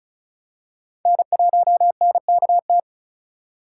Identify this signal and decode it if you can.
Morse code, heard as a keyed tone, about 36 wpm, 700 Hz. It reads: N1NKT
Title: USA Call Signs - 5 Characters - 2x Character Spacing - 35wpm